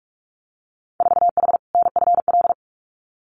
Morse code, heard as a keyed tone, about 46 wpm, 700 Hz. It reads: 4H NFL
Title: Winter Field Day Exchanges Encore 45wpm